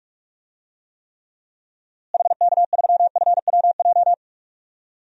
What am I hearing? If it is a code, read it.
Morse code, heard as a keyed tone, about 45 wpm, 700 Hz. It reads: HK3FWJ
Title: DXCC-100 Call Signs - 6 Characters - Repeated 2x - 45wpm